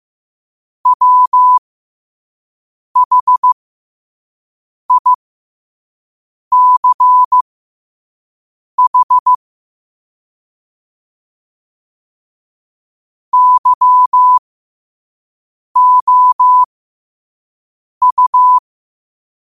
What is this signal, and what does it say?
Morse code, heard as a keyed tone, about 15 wpm, spaced as for 5 wpm, 1000 Hz. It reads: WHICH YOU
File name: qotdfiles / 5wpm